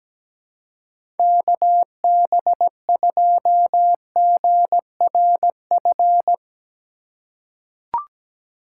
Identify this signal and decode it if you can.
Morse code, heard as a keyed tone, about 17 wpm, 700 Hz. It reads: KB2GRF
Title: USA Call Signs - 6 Characters - 3x Character Spacing - 17wpm